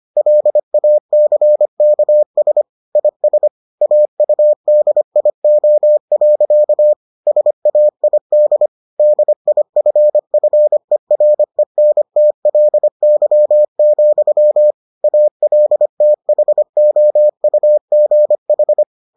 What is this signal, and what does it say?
Morse code, heard as a keyed tone, about 25 wpm, 600 Hz. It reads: LACKS IS AUDIO. SAID DIFFERENTLY, ALTHOUGH